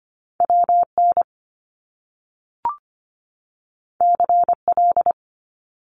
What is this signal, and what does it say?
Morse code, heard as a keyed tone, about 25 wpm, 700 Hz. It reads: WN CL
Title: Top Two Letter Combinations in Words 25wpm